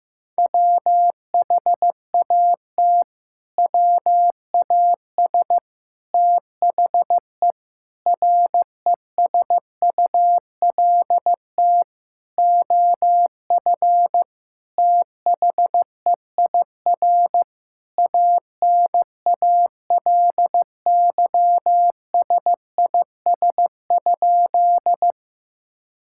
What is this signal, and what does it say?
Morse code, heard as a keyed tone, about 16 wpm, 700 Hz. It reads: WHAT WAS THE RESULT OF THEIR ANALYSIS?
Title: Sentences from Top 1000 Words - Review Part 5 - 15wpm